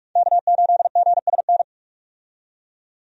Morse code, heard as a keyed tone, about 43 wpm, 700 Hz. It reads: K8GSN
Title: USA Call Signs - 5 Characters - 45wpm